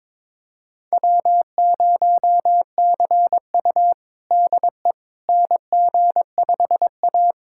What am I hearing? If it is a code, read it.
Morse code, heard as a keyed tone, about 22 wpm, 700 Hz. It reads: W0CU DE NG5A
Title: USA Callsign DE Callsign - 4 Characters - 22wpm